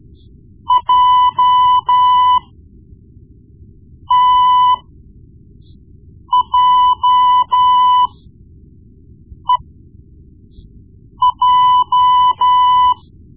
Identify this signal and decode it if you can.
Morse code, heard as a keyed tone, about 13 wpm, spaced as for 5 wpm, 1000 Hz. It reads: JTJEJ
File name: Morse Lessons / Morse Lessons / Tape-1B